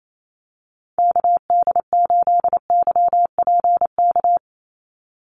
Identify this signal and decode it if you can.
Morse code, heard as a keyed tone, about 28 wpm, 700 Hz. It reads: KD8YPK
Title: USA Call Signs 28wpm